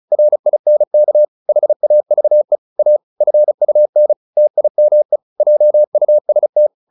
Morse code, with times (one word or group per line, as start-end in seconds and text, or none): 0.1-1.2 RINK
1.5-2.6 HAVE
2.8-3.0 A
3.2-4.1 FUN
4.4-5.2 TIME
5.4-6.7 JUST